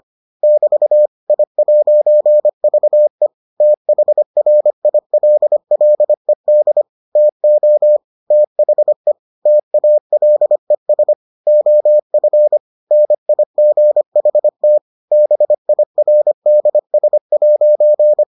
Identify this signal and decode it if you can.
Morse code, heard as a keyed tone, about 25 wpm, 600 Hz. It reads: <BT> I'VE THRILLED TO THE TALES OF NIGHT BIRDS'